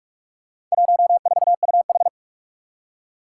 Morse code, heard as a keyed tone, about 46 wpm, 700 Hz. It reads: 14UH
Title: Four Challenging Letter-Number Combinations 45wpm